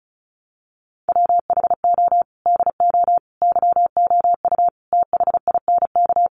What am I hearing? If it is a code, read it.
Morse code, heard as a keyed tone, about 34 wpm, 700 Hz. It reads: WHO DO YOU THINK
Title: Sentences from Top 1100 Words 35wpm